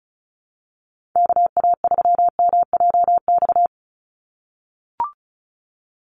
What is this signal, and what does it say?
Morse code, heard as a keyed tone, about 34 wpm, 700 Hz. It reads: KA3MJX
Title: USA Call Signs 35wpm